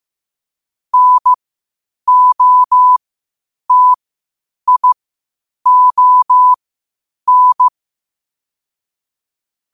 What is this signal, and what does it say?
Morse code, heard as a keyed tone, about 15 wpm, spaced as for 8 wpm, 1000 Hz. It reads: NOTION
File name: qotdfiles / 8wpm